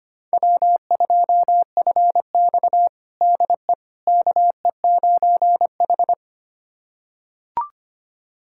Morse code, heard as a keyed tone, about 25 wpm, 700 Hz. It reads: W2FX DE KE9H
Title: USA Callsign DE Callsign - 4 Characters - 25wpm